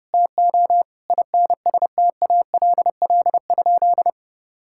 Morse code, heard as a keyed tone, about 30 wpm, 700 Hz. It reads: TO INSTALL?
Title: Sentences from Top 1600 Words 30wpm